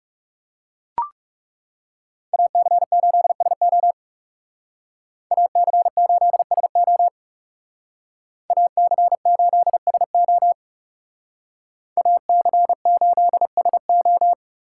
Morse code, heard as a keyed tone, about 36 wpm, 700 Hz. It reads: AC8SO AC8SO AC8SO AC8SO
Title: Speed Racing - USA 2x2 Call Signs 30wpm